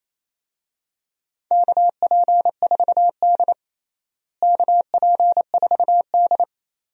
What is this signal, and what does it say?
Morse code, heard as a keyed tone, about 28 wpm, 700 Hz. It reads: KP4D KP4D
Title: DXCC-100 Call Signs - 4 Characters - Repeated 2x - 28wpm